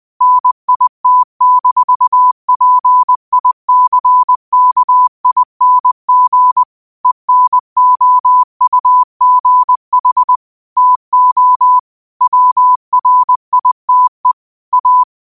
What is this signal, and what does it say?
Morse code, heard as a keyed tone, about 20 wpm, 1000 Hz. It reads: NIT-PICKING ENOUGH TO WRITE A
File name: qotdfiles / 20wpm